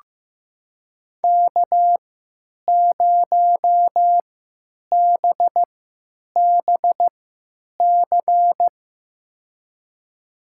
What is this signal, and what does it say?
Morse code, heard as a keyed tone, about 14 wpm, spaced as for 8 wpm, 700 Hz. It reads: K0BBC
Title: USA Call Signs - 5 Characters - 3x Character Spacing - 15wpm